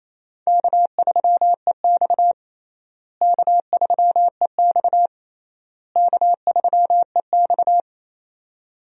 Morse code, heard as a keyed tone, about 28 wpm, 700 Hz. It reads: K3EX K3EX K3EX
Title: USA Call Signs - 4 Characters - Repeated 3x - 28wpm